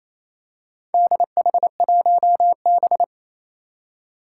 Morse code, heard as a keyed tone, about 28 wpm, 700 Hz. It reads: DH1B
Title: Four Challenging Letter-Number Combinations 28wpm